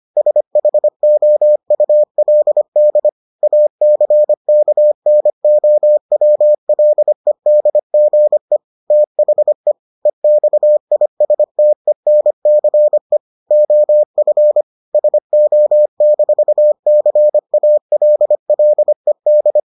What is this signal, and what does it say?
Morse code, heard as a keyed tone, about 25 wpm, 600 Hz. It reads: SHOULD ACKNOWLEDGE THE EXISTENCE OF SO-CALLED